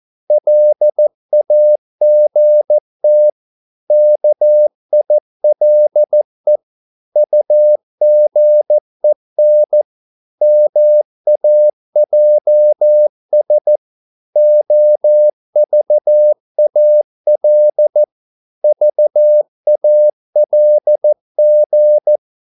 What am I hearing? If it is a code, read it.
Morse code, heard as a keyed tone, about 14 wpm, 600 Hz. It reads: LAGT KILE UGEN MAJS OVAL VALG
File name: Korte ord DK 14wpm